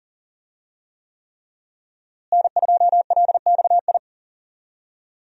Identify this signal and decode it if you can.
Morse code, heard as a keyed tone, about 40 wpm, 700 Hz. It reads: N2LXI
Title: USA Call Signs - 5 Characters - Repeated 3x - 40wpm